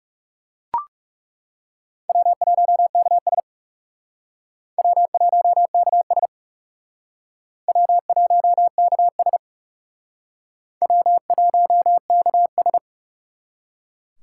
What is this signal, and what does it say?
Morse code, heard as a keyed tone, about 36 wpm, 700 Hz. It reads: W1KS W1KS W1KS W1KS
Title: Speed Racing - USA 1x2 Call Signs 30wpm